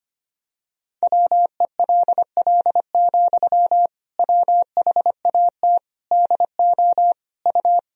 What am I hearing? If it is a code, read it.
Morse code, heard as a keyed tone, about 25 wpm, 700 Hz. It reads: WELL, WHAT DO U
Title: Sentences from Top 200 Words and Abbreviations 25wpm